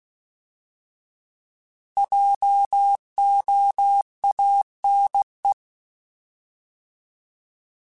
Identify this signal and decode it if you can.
Morse code, heard as a keyed tone, about 16 wpm, 800 Hz. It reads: JOANE
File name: Js @15